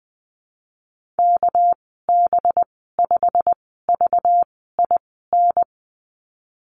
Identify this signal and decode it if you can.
Morse code, heard as a keyed tone, about 20 wpm, spaced as for 14 wpm, 700 Hz. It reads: KB5VIN
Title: 238 Lesson 38g - Letter q - Call Signs 20wpm